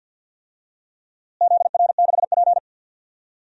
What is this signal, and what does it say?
Morse code, heard as a keyed tone, about 51 wpm, 700 Hz. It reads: ZR6P
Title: DXCC-100 Call Signs - 4 Characters - Repeated 3x - 50wpm